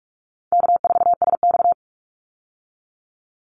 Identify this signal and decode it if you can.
Morse code, heard as a keyed tone, about 44 wpm, 700 Hz. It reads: K4SX
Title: Call Signs Repeated 2x - 45wpm